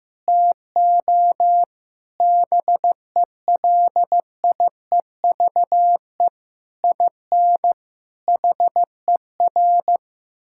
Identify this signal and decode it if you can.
Morse code, heard as a keyed tone, about 15 wpm, 700 Hz. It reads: TO BELIEVE IN HER
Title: Sentences from Top 1600 Words 15wpm